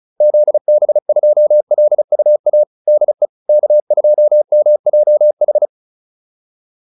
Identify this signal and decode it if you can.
Morse code, heard as a keyed tone, about 34 wpm, 600 Hz. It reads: ZB2LUA DE K2MJH
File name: QSO 402 35